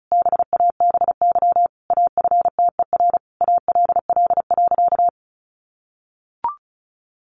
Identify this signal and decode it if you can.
Morse code, heard as a keyed tone, about 34 wpm, 700 Hz. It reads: BABY AFTER ALL.
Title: Sentences from Top 1300 Words 35wpm